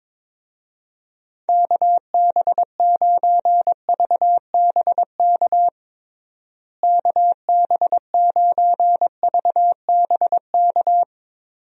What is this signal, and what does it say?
Morse code, heard as a keyed tone, about 22 wpm, 700 Hz. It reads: KB9VBK KB9VBK
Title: USA Call Signs - 6 Characters - Repeated 2x - 22wpm